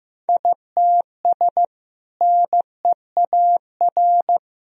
Morse code, heard as a keyed tone, about 15 wpm, 700 Hz. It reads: ITS NEAR
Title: Sentences from Top 1000 Words - Review Part 2 - 15wpm